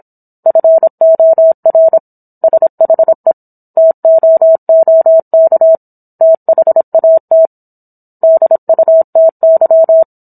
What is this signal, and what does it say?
Morse code, heard as a keyed tone, about 27 wpm, 650 Hz. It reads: FOR SHE TOOK THAT DUTY